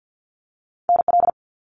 Morse code, heard as a keyed tone, about 50 wpm, 700 Hz. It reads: NL
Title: Canadian Provinces and Territories - Repeated 3x - 50wpm